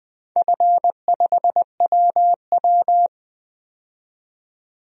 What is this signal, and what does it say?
Morse code, heard as a keyed tone, about 21 wpm, 700 Hz. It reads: F5WW
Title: DXCC-100 Call Signs - 4 Characters - 20wpm